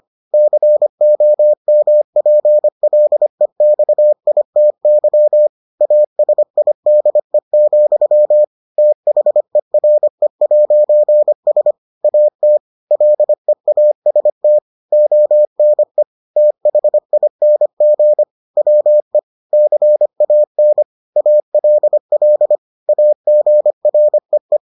Morse code, heard as a keyed tone, about 25 wpm, 600 Hz. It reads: COMPLEXITY ASIDE, THERE'S AT LEAST ONE THING WE CAN ALL AGREE